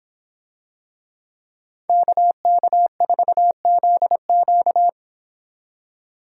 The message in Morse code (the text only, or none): KK4ZQ